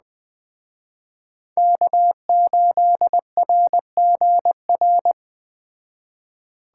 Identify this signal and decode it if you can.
Morse code, heard as a keyed tone, about 20 wpm, 700 Hz. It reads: K8RGR